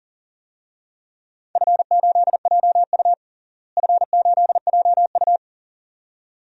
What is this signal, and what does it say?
Morse code, heard as a keyed tone, about 40 wpm, 700 Hz. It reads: F8JU F8JU
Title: DXCC-100 Call Signs - 4 Characters - Repeated 2x - 40wpm